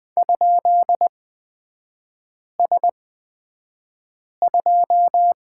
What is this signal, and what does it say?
Morse code, heard as a keyed tone, about 20 wpm, spaced as for 5 wpm, 700 Hz. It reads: ?S2
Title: 229 Lesson 37e - Letter x - Three Characters 20wpm